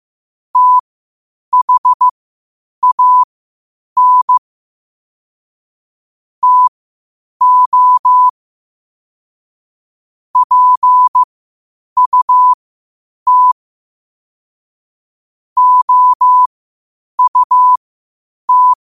Morse code, heard as a keyed tone, about 15 wpm, spaced as for 8 wpm, 1000 Hz. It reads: THAN TO PUT OUT
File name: qotdfiles / 8wpm